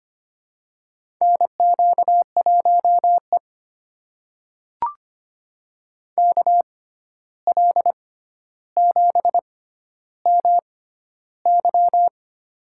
Morse code, heard as a keyed tone, about 25 wpm, spaced as for 12 wpm, 700 Hz. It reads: NQ1E KL7MY
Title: USA Call Signs - 6x Character Spacing - 25wpm